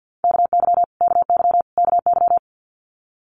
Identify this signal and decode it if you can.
Morse code, heard as a keyed tone, about 50 wpm, 700 Hz. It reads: KY KY KY
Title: Multi-letter Phonemes Repeated 3x 50wpm